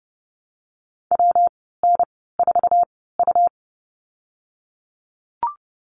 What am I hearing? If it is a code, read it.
Morse code, heard as a keyed tone, about 30 wpm, spaced as for 17 wpm, 700 Hz. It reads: WN4U
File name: Lesson 19g - Letter w - Call Signs 30wpm